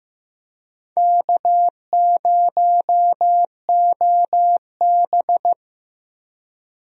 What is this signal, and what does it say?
Morse code, heard as a keyed tone, about 15 wpm, 700 Hz. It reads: K0OB
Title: USA Call Signs - 4 Characters - Repeated 3x - 15wpm